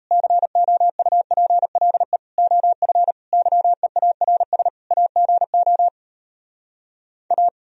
Morse code, heard as a keyed tone, about 38 wpm, 700 Hz. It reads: COUPLE OF YEARS AGO A